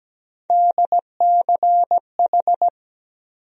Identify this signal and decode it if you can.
Morse code, heard as a keyed tone, about 17 wpm, 700 Hz. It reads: DCH
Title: USA Call Sign Postfixes - 3 Letters 17wpm